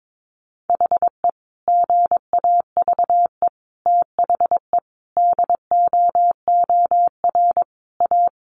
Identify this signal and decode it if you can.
Morse code, heard as a keyed tone, about 22 wpm, 700 Hz. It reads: HE GAVE THE DOOR A
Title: Sentences from Top 1300 Words 22wpm